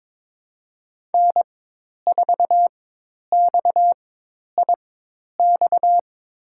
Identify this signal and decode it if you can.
Morse code, heard as a keyed tone, about 22 wpm, spaced as for 10 wpm, 700 Hz. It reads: N4XIX
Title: USA Call Signs - 5 Characters - 4x Character Spacing - 22wpm